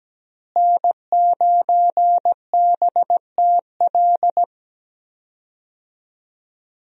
Morse code, heard as a keyed tone, about 17 wpm, 700 Hz. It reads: N9BTL
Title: USA Call Signs - 5 Characters - Repeated 3x - 17wpm